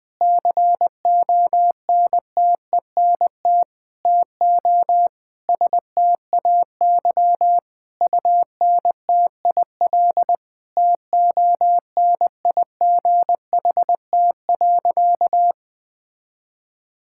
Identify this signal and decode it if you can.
Morse code, heard as a keyed tone, about 20 wpm, 700 Hz. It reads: CONTENT TO STAY UNTIL TONIGHT.
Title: Sentences from Top 1000 Words - Review Part 6 - 20wpm